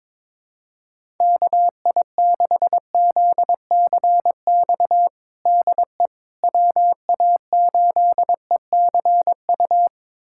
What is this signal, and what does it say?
Morse code, heard as a keyed tone, about 22 wpm, 700 Hz. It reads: KI6ZCX DE WA8ECU
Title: USA Callsign DE Callsign - 6 Characters - 22wpm